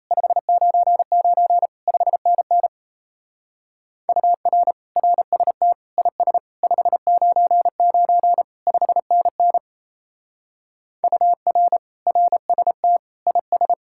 Morse code, heard as a keyed tone, about 34 wpm, 700 Hz. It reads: 599 5NN UR RST IS 599 5NN UR RST IS
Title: Speed-Racing Common QSO Phrases - 28wpm